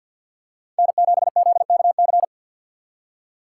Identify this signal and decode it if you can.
Morse code, heard as a keyed tone, about 49 wpm, 700 Hz. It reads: N7ZKC
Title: USA Call Signs - 5 Characters - 2x Character Spacing - 50wpm